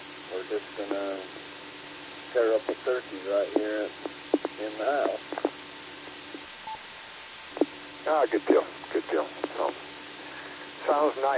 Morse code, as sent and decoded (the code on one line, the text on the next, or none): none